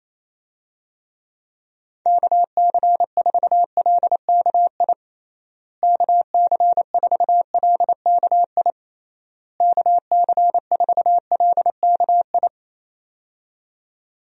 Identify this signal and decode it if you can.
Morse code, heard as a keyed tone, about 28 wpm, 700 Hz. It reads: KC4LKI KC4LKI KC4LKI
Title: USA Call Signs - 6 Characters - Repeated 3x - 28wpm